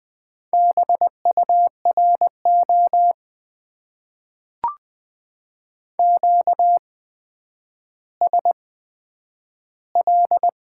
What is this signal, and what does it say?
Morse code, heard as a keyed tone, about 20 wpm, spaced as for 10 wpm, 700 Hz. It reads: BURO QSL
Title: Top 90 QSO Elements - 8x Character Spacing - 20wpm